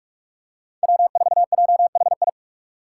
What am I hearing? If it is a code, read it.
Morse code, heard as a keyed tone, about 45 wpm, 700 Hz. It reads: W4JHI
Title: Five Challenging Letter-Number Combinations 45wpm